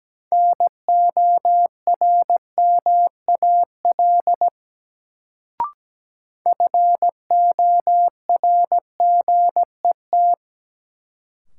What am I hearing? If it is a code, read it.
Morse code, heard as a keyed tone, about 17 wpm, 700 Hz. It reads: NORMAL FORGET
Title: Sentences from Top 1200 Words - New Word Drill - 17wpm